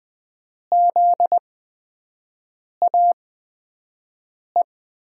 Morse code, heard as a keyed tone, about 20 wpm, spaced as for 5 wpm, 700 Hz. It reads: ZAE
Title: USA Call Signs - 6 Characters - 8x Character Spacing - 20wpm